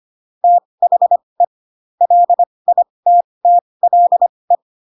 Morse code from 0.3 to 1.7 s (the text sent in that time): THE